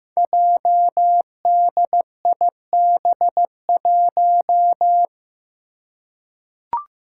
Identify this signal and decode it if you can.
Morse code, heard as a keyed tone, about 15 wpm, 700 Hz. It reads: JDIB1
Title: Five Challenging Letter-Number Combinations 15wpm